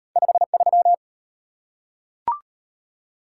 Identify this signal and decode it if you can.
Morse code, heard as a keyed tone, about 34 wpm, 700 Hz. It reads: 53